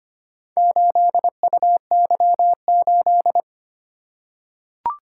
Morse code, heard as a keyed tone, about 25 wpm, 700 Hz. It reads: 8UY8
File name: practice / 25wpm / Four Letter-Number Combinations - 25wpm